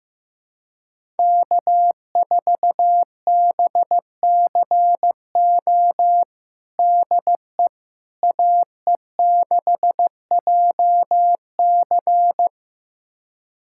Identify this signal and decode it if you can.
Morse code, heard as a keyed tone, about 15 wpm, 700 Hz. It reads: K4BCO DE AE6JC
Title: USA Callsign DE Callsign - 5 Characters - 15wpm